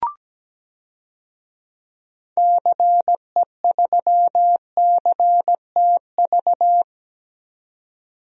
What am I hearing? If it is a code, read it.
Morse code, heard as a keyed tone, about 17 wpm, 700 Hz. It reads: CE3CTV
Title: DXCC-100 Call Signs - 6 Characters - 17wpm